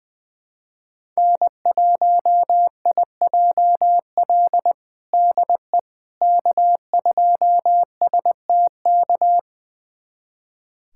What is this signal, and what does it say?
Morse code, heard as a keyed tone, about 20 wpm, 700 Hz. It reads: N1IJL DE K2STK
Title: USA Callsign DE Callsign - 5 Characters - 20wpm